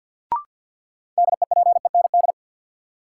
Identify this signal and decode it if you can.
Morse code, heard as a keyed tone, about 53 wpm, 700 Hz. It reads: DEPEND
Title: Sentences from Top 1300 Words - New Word Drill Alt - 50wpm